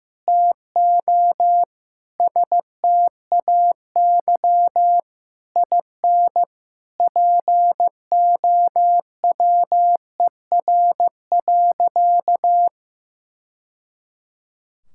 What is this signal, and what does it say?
Morse code, heard as a keyed tone, about 15 wpm, 700 Hz. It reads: TO STAY IN POWER.
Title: Sentences from Top 1000 Words - Review Part 5 - 15wpm